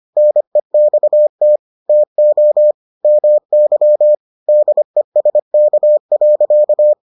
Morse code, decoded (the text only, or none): NEXT TO MY DESK.